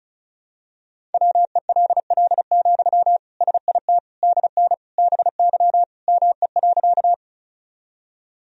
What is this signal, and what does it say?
Morse code, heard as a keyed tone, about 36 wpm, 700 Hz. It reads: WELL, SIT DN BY ME.
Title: Sentences from Top 500 Words and Abbreviations 35wpm